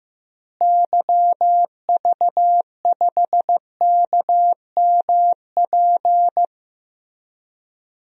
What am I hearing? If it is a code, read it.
Morse code, heard as a keyed tone, about 15 wpm, 700 Hz. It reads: YV5KMP
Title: DXCC-100 Call Signs - 6 Characters - Repeated 2x - 15wpm